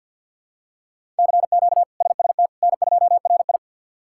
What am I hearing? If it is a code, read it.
Morse code, heard as a keyed tone, about 53 wpm, 700 Hz. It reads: CQ SST N2RI
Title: K1USN SST QSO Practice - 50wpm